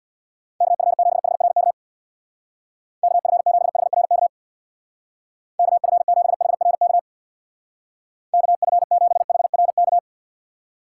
Morse code, heard as a keyed tone, about 63 wpm, 700 Hz. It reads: KF7HRK KF7HRK KF7HRK KF7HRK